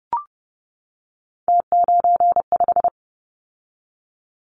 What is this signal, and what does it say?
Morse code, heard as a keyed tone, about 30 wpm, 700 Hz. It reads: T95
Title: Callsign Prefixes for Top 50 DXCC - 30wpm